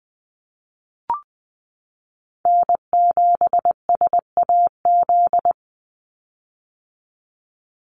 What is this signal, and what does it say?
Morse code, heard as a keyed tone, about 20 wpm, 700 Hz. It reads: N7SAZ
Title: USA Call Signs 20wpm